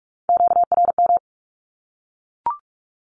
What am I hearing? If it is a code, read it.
Morse code, heard as a keyed tone, about 44 wpm, 700 Hz. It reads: QRM
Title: Q-Codes 45wpm